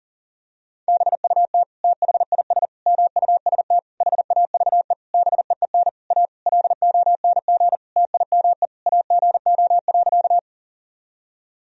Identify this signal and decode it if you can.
Morse code, heard as a keyed tone, about 40 wpm, 700 Hz. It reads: BUT THIS MUST HAVE BEEN A LONG TIME AGO.
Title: Sentences from Top 1000 Words - Review Part 3 - 40wpm